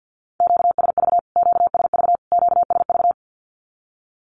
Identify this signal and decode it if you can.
Morse code, heard as a keyed tone, about 50 wpm, 700 Hz. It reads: QSV QSV QSV
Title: Q-Codes Repeated 3x 50wpm